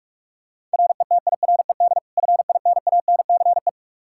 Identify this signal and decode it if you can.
Morse code, heard as a keyed tone, about 46 wpm, 700 Hz. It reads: RETIRED FINANCE
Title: Practice QSO - Exchange 3 - More Info - 45wpm